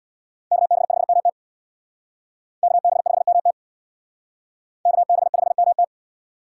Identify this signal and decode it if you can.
Morse code, heard as a keyed tone, about 65 wpm, 700 Hz. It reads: KB5MT KB5MT KB5MT